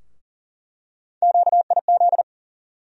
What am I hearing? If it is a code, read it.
Morse code, heard as a keyed tone, about 40 wpm, 700 Hz. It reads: QIZ
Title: USA Call Sign Postfixes - 3 Letters 40wpm